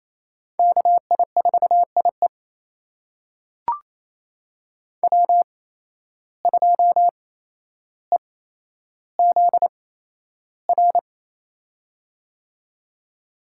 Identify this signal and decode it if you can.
Morse code, heard as a keyed tone, about 28 wpm, spaced as for 12 wpm, 700 Hz. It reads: KI4IE W2EZR
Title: USA Call Signs - 5 Characters - 8x Character Spacing - 28wpm